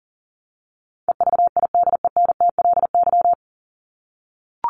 Morse code, heard as a keyed tone, about 40 wpm, 700 Hz. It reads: EVIDENTLY
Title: LY Suffix - Top 100 Words with Base Words 40wpm